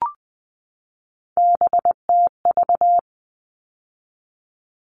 Morse code, heard as a keyed tone, about 20 wpm, 700 Hz. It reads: BTV
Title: All Cut Numbers - 3-Digit 20wpm